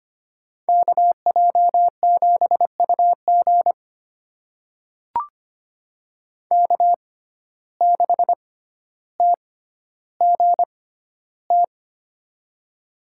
Morse code, heard as a keyed tone, about 25 wpm, spaced as for 13 wpm, 700 Hz. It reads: KJ7UG K6TGT
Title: USA Call Signs - 5 Characters - 6x Character Spacing - 25wpm